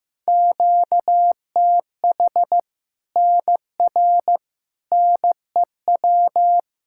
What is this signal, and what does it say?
Morse code, heard as a keyed tone, about 15 wpm, 700 Hz. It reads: QTH NR NEW
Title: K1USN SST QSO Practice 15wpm